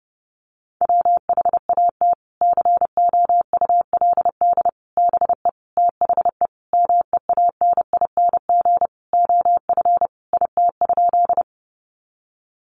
Morse code, heard as a keyed tone, about 30 wpm, 700 Hz. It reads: WHAT COULD BE THE MEANING OF IT?
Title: Sentences from Top 900 Words 30wpm